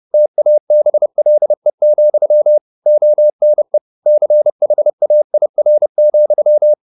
Morse code, heard as a keyed tone, about 30 wpm, 600 Hz. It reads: TABLE, ONE CHAIR,
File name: the old man and the sea / the old man and the sea 1 30wpm